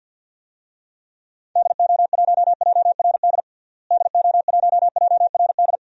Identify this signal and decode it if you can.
Morse code, heard as a keyed tone, about 52 wpm, 700 Hz. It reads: DO1JRD DO1JRD
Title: DXCC-100 Call Signs - 6 Characters - Repeated 3x - 50wpm